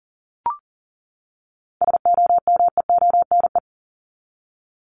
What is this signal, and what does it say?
Morse code, heard as a keyed tone, about 40 wpm, 700 Hz. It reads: SOMEONE
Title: Top 501-750 Words 40wpm